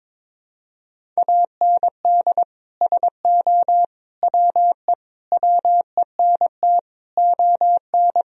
Morse code, heard as a keyed tone, about 22 wpm, 700 Hz. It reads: AND SO WE WENT ON
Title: Sentences from Top 1000 Words - Review Part 1 - 22wpm